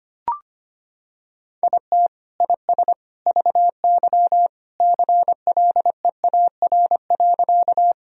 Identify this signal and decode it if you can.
Morse code, heard as a keyed tone, about 25 wpm, 700 Hz. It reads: IT IS VY CLEAR.
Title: Sentences from Top 500 Words and Abbreviations 25wpm